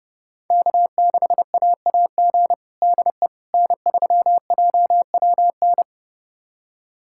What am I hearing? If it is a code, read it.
Morse code, heard as a keyed tone, about 32 wpm, 700 Hz. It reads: K6AAG DE N3JWN
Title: USA Callsign DE Callsign - 5 Characters - 30wpm